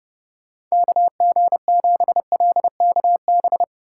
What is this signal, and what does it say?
Morse code, heard as a keyed tone, about 37 wpm, 700 Hz. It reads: KG7LKB
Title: Speed Racing - USA 2x3 Call Signs 30wpm